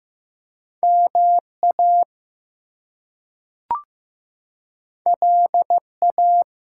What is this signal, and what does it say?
Morse code, heard as a keyed tone, about 15 wpm, 700 Hz. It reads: MA LA
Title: US States and Canadian Provinces 15wpm